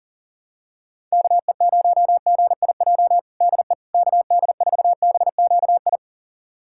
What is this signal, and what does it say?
Morse code, heard as a keyed tone, about 40 wpm, 700 Hz. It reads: KE0GIJ DE KD4BQI
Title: USA Callsign DE Callsign - 6 Characters - 40wpm